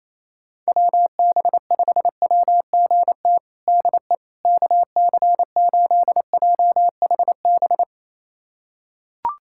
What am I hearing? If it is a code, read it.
Morse code, heard as a keyed tone, about 28 wpm, 700 Hz. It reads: WB5WGT DE KC8JHB
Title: USA Callsign DE Callsign 28wpm